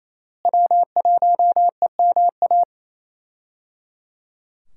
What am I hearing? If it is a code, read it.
Morse code, heard as a keyed tone, about 28 wpm, 700 Hz. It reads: W1EMA
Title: USA Call Signs - 5 Characters - Repeated 3x - 28wpm